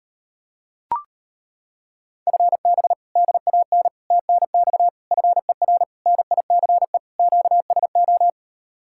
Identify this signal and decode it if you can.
Morse code, heard as a keyed tone, about 32 wpm, 700 Hz. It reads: FB DAN TNX FER NICE QSO